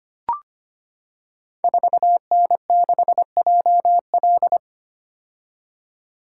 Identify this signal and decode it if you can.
Morse code, heard as a keyed tone, about 25 wpm, 700 Hz. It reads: 4N6JL
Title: Code Groups - 5 Letter:Numbers - 25wpm